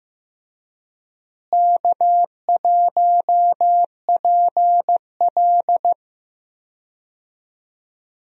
Morse code, heard as a keyed tone, about 15 wpm, 700 Hz. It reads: K1PL
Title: USA Call Signs - 4 Characters - Repeated 2x - 15wpm